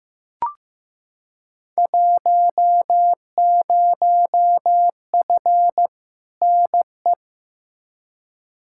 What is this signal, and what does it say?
Morse code, heard as a keyed tone, about 15 wpm, 700 Hz. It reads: 10F NE
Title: Field Day Exchanges Encore - 15wpm